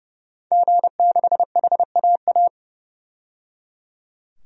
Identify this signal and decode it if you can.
Morse code, heard as a keyed tone, about 30 wpm, 700 Hz. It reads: G6HAA
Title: DXCC-100 Call Signs - 5 Characters - 30wpm